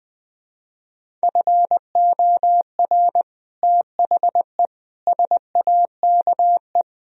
Morse code, heard as a keyed tone, about 20 wpm, 700 Hz. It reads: FOR THE SAKE
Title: Speed-Racing - Sets of 4 Words 20wpm